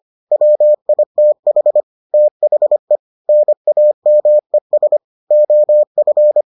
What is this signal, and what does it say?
Morse code, heard as a keyed tone, about 25 wpm, 600 Hz. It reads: WITH THE NAMES OF